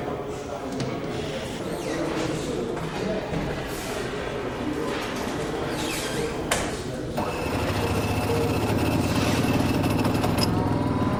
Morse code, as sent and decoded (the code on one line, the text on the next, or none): none